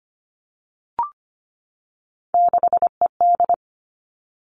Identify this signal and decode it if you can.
Morse code, heard as a keyed tone, about 25 wpm, 700 Hz. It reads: 6ED
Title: All Cut Numbers - 3-Digit 25wpm